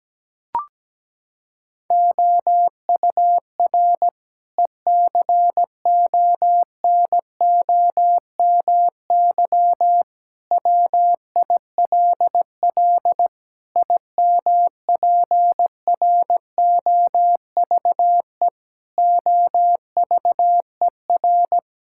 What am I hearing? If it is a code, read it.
Morse code, heard as a keyed tone, about 17 wpm, 700 Hz. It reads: OUR ECONOMY WILL IMPROVE OVER
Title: Sentences from Top 1000 Words - Review Part 5 - 17wpm